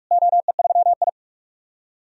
Morse code, heard as a keyed tone, about 46 wpm, 700 Hz. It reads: OE3I
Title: DXCC-100 Call Signs - 4 Characters - Repeated 2x - 45wpm